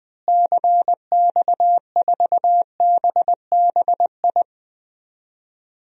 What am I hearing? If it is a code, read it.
Morse code, heard as a keyed tone, about 21 wpm, 700 Hz. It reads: CX4BBI
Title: DXCC-100 Call Signs - 6 Characters - Repeated 2x - 20wpm